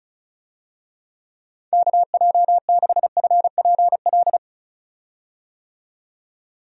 Morse code, heard as a keyed tone, about 34 wpm, 700 Hz. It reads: KJ6FPL
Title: USA Call Signs - 6 Characters - 35wpm